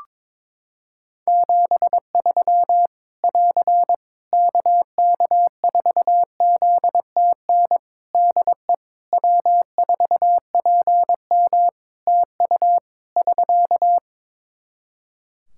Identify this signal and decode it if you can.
Morse code, heard as a keyed tone, about 22 wpm, 700 Hz. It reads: 73 <AR> KK4ZTN DE W4PM TU <SK>
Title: Practice QSO - Exchange 4 - Ending - 22wpm